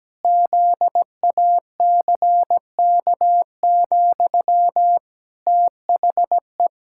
Morse code, heard as a keyed tone, about 17 wpm, 700 Hz. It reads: ZACK, THE
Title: Pangram Challenge v2 - 17wpm